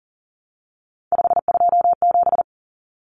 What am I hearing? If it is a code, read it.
Morse code, heard as a keyed tone, about 40 wpm, 700 Hz. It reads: 527
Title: Numbers - 3-Digit 40wpm